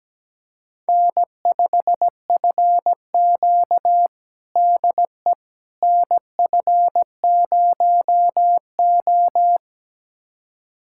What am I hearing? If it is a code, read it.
Morse code, heard as a keyed tone, about 17 wpm, 700 Hz. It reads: N5FQ DE NF0O
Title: USA Callsign DE Callsign - 4 Characters - 17wpm